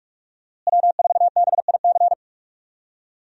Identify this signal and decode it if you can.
Morse code, heard as a keyed tone, about 42 wpm, 700 Hz. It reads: W4BIC
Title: USA Call Signs - 5 Characters - Repeated 3x - 45wpm